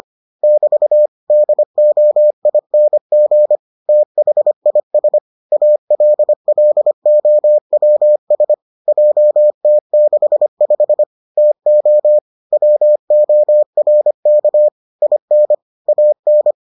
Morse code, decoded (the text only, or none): <BT> DOING THIS ALLOWS JT65 TO WORK IN AN